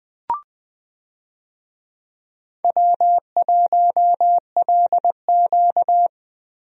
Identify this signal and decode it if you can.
Morse code, heard as a keyed tone, about 20 wpm, 700 Hz. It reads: W1LQ
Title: USA Call Signs - 4 Characters - Repeated 3x - 20wpm